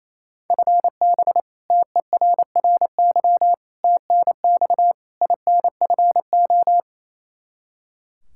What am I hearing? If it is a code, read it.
Morse code, heard as a keyed tone, about 27 wpm, 700 Hz. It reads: FB TERRY TNX INFO
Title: Practice QSO - Exchange 3 - More Info - 28wpm